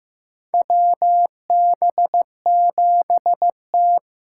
Morse code, heard as a keyed tone, about 15 wpm, 700 Hz. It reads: WB7T
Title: USA Call Signs - 6 Characters - 6x Character Spacing - 15wpm